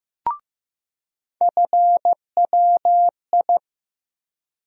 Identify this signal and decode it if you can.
Morse code, heard as a keyed tone, about 15 wpm, 700 Hz. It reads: FWI
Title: USA Call Sign Postfixes - 3 Letters 15wpm